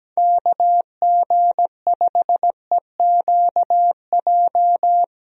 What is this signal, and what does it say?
Morse code, heard as a keyed tone, about 17 wpm, 700 Hz. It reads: KG5EQJ
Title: USA Call Signs - 6 Characters - Repeated 3x - 17wpm